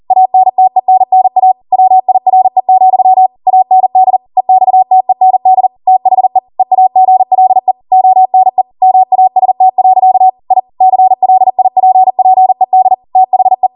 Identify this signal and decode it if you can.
Morse code, heard as a keyed tone, about 40 wpm, 750 Hz. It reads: ANTENNA WIRE, AND EXTEND THE EAGLE ONE MAST. I CLIPPED THE